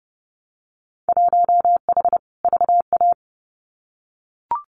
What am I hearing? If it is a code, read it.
Morse code, heard as a keyed tone, about 30 wpm, 700 Hz. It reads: 1H VA
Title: Winter Field Day Exchanges 30wpm